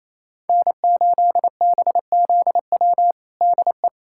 Morse code, heard as a keyed tone, about 28 wpm, 700 Hz. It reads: N8BZW DE
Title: USA Callsign DE Callsign 28wpm